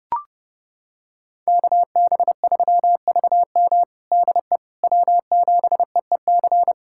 Morse code, heard as a keyed tone, about 32 wpm, 700 Hz. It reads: KB3VM DE W7EEC
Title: USA Callsign DE Callsign - 5 Characters - 30wpm